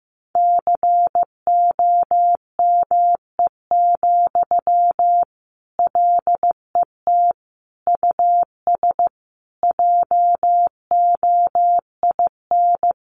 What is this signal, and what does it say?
Morse code, heard as a keyed tone, about 15 wpm, 700 Hz. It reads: COME, LET US JOIN
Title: Sentences from Top 1400 Words 15wpm